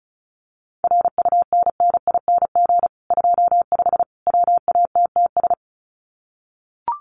Morse code, heard as a keyed tone, about 34 wpm, 700 Hz. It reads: RUNNING 25 WATTS
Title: Common QSO Phrases 35wpm